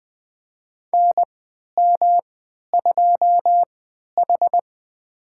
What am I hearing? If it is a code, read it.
Morse code, heard as a keyed tone, about 20 wpm, spaced as for 11 wpm, 700 Hz. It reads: NM2H
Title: USA Call Signs - 3x Character Spacing - 20wpm